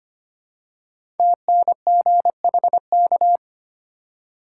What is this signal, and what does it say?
Morse code, heard as a keyed tone, about 25 wpm, 700 Hz. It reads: TNGHK
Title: Code Groups - 5 Letter:Numbers - 25wpm